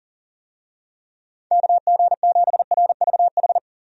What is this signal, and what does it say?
Morse code, heard as a keyed tone, about 40 wpm, 700 Hz. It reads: KG7RVH
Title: USA Call Signs - 6 Characters - 40wpm